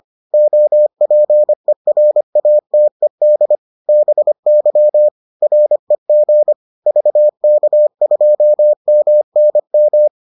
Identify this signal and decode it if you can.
Morse code, heard as a keyed tone, about 25 wpm, 600 Hz. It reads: OPERATED BY REG VK2MNM